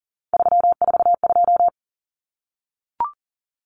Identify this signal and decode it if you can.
Morse code, heard as a keyed tone, about 40 wpm, 700 Hz. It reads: 342
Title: Numbers - 3-Digit 40wpm